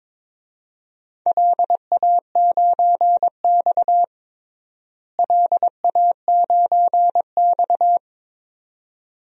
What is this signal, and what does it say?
Morse code, heard as a keyed tone, about 22 wpm, 700 Hz. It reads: LA9X LA9X
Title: DXCC-100 Call Signs - 4 Characters - Repeated 2x - 22wpm